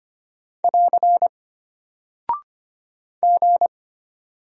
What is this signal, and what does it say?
Morse code, heard as a keyed tone, about 25 wpm, 700 Hz. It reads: <AR> G
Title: Single Character (Letter:Number:Punctuation:Prosign) Alt 25wpm